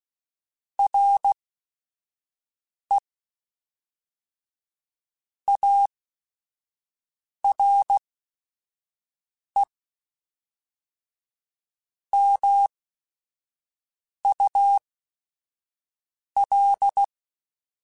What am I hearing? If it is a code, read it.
Morse code, heard as a keyed tone, about 16 wpm, spaced as for 5 wpm, 800 Hz. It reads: RE ARE MUL
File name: Technical Subjects / Technical Subjects-mp3s / AND Gate @5